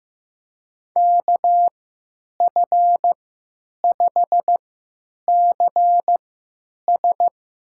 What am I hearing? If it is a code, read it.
Morse code, heard as a keyed tone, about 15 wpm, spaced as for 9 wpm, 700 Hz. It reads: KF5CS
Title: USA Call Signs - 5 Characters - 3x Character Spacing - 15wpm